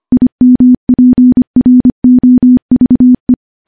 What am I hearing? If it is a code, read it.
Morse code, heard as a keyed tone, about 25 wpm, 250 Hz. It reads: IMPROVE
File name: morse / morse-training-25WPM